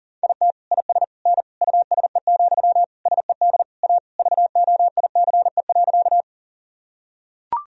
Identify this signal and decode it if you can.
Morse code, heard as a keyed tone, about 40 wpm, 700 Hz. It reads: IT IS N USE, SED A VOICE.
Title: Sentences from Top 500 Words and Abbreviations 40wpm